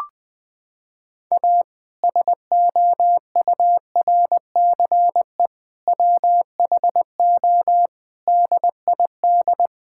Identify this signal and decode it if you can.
Morse code, heard as a keyed tone, about 20 wpm, 700 Hz. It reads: A SOURCE WHO DID
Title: Sentences with 8-Word Phrases - 20wpm